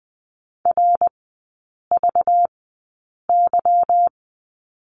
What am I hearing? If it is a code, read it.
Morse code, heard as a keyed tone, about 20 wpm, spaced as for 8 wpm, 700 Hz. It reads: RVY
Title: 191 Lesson 31e - Letter v - Three Characters 20wpm